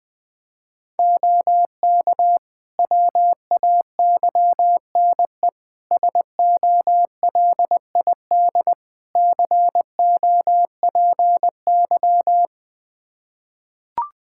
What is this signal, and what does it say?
Morse code, heard as a keyed tone, about 20 wpm, 700 Hz. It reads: OK WAYNE SOLID COPY
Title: Practice QSO - Exchange 3 - More Info - 20wpm